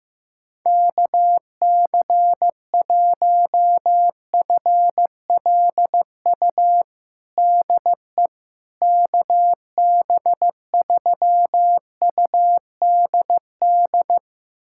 Morse code, heard as a keyed tone, about 15 wpm, 700 Hz. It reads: KC1FLU DE KB3UDD
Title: USA Callsign DE Callsign - 6 Characters - 15wpm